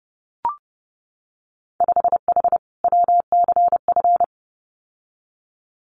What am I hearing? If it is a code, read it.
Morse code, heard as a keyed tone, about 30 wpm, 700 Hz. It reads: 5H WCF
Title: Winter Field Day Exchanges Encore 30wpm